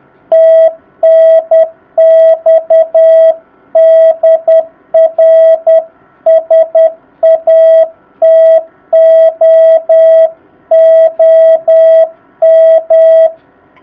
Morse code, heard as a keyed tone, about 10 wpm, 660 Hz. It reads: TNX DR SATO OM
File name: jh8cbh / 2019cw